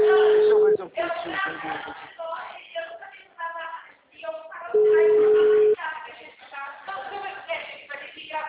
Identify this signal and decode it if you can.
Ringback tone (Europe, 425 Hz), with speech.